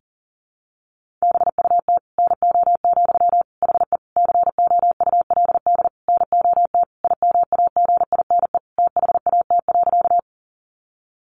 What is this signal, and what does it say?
Morse code, heard as a keyed tone, about 37 wpm, 700 Hz. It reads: BUT NO, HE COULD NOT IMAGINE THAT.
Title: Sentences from Top 1400 Words 40wpm